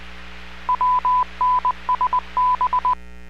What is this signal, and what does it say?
Morse code, heard as a keyed tone, about 20 wpm, 1000 Hz. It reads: WNSB